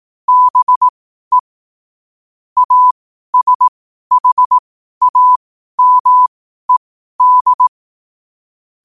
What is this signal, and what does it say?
Morse code, heard as a keyed tone, about 18 wpm, spaced as for 12 wpm, 1000 Hz. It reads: BE ASHAMED
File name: qotdfiles / 12wpm